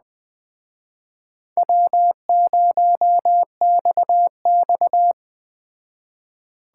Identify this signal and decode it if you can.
Morse code, heard as a keyed tone, about 21 wpm, 700 Hz. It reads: W0XX
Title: Speed Racing - USA 1x2 Call Signs 17wpm